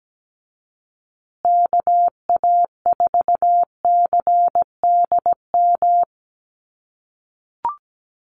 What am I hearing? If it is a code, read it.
Morse code, heard as a keyed tone, about 17 wpm, 700 Hz. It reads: KA4CDM
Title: USA Call Signs Repeated 2x 17wpm